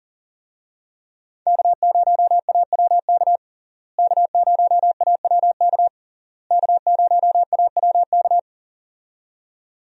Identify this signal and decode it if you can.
Morse code, heard as a keyed tone, about 40 wpm, 700 Hz. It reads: K0AWK K0AWK K0AWK
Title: USA Call Signs - 5 Characters - Repeated 3x - 40wpm